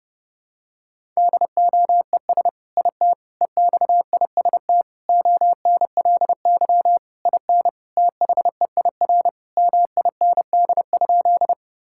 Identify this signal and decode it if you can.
Morse code, heard as a keyed tone, about 30 wpm, 700 Hz. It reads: DOES IT EXIST ONLY IN THEIR MIND?
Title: Sentences from Top 1000 Words - Review Part 5 - 30wpm